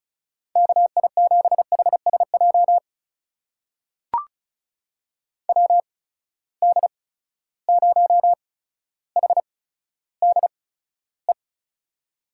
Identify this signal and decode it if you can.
Morse code, heard as a keyed tone, about 36 wpm, spaced as for 15 wpm, 700 Hz. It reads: KI7HSJ WD0HDE
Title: USA Call Signs - 6 Characters - 8x Character Spacing - 35wpm